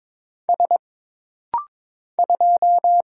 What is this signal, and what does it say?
Morse code, heard as a keyed tone, about 22 wpm, 700 Hz. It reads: S 2
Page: This rapid-fire timing is geared to push your ICR (Instant Character Recognition) skills.